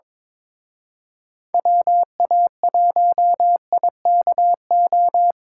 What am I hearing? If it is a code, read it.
Morse code, heard as a keyed tone, about 25 wpm, 700 Hz. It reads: WA1IKO
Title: Speed Racing - USA 2x3 Call Signs 22wpm